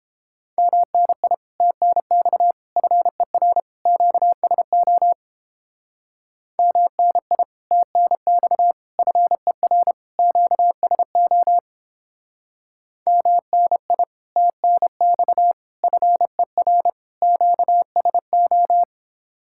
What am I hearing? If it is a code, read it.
Morse code, heard as a keyed tone, about 29 wpm, 700 Hz. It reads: MNI TNX FER QSO MNI TNX FER QSO MNI TNX FER QSO